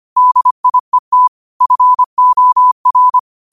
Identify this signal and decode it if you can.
Morse code, heard as a keyed tone, about 25 wpm, 1000 Hz. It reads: DIET FOR
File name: qotdfiles / 25wpm